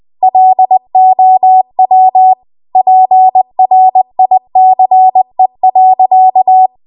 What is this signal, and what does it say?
Morse code, heard as a keyed tone, about 20 wpm, 750 Hz. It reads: LOW PRICE.
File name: Archive / 20 WPM / 20WPM